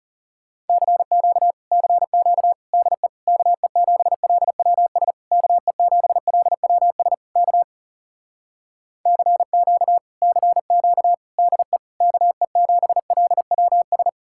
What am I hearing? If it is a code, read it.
Morse code, heard as a keyed tone, about 40 wpm, 700 Hz. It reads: CQ CQ DE KE7LWS KE7LWS K CQ CQ DE KE7LWS